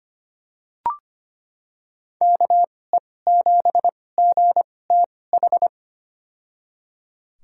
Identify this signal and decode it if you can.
Morse code, heard as a keyed tone, about 25 wpm, spaced as for 18 wpm, 700 Hz. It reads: KE7GTH
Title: USA Call Signs - 6 Characters - 2x Character Spacing - 25wpm